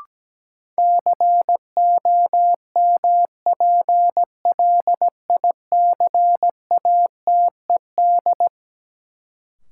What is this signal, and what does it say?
Morse code, heard as a keyed tone, about 17 wpm, 700 Hz. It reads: COMPLICATED
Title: Sentences from Top 1200 Words - New Word Drill - 17wpm